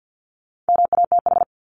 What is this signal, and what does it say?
Morse code, heard as a keyed tone, about 50 wpm, 700 Hz. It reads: MATH
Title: Top 751-1000 Words 50wpm